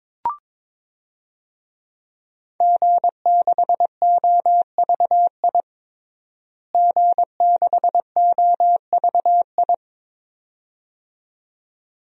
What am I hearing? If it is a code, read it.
Morse code, heard as a keyed tone, about 22 wpm, 700 Hz. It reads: G6OVI G6OVI
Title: DXCC-100 Call Signs - 5 Characters - Repeated 2x - 22wpm